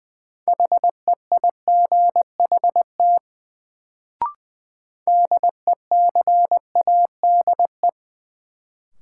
Morse code, heard as a keyed tone, about 20 wpm, 700 Hz. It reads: HEIGHT DECADE
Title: Sets of 5 Words - New Word Drill - 20wpm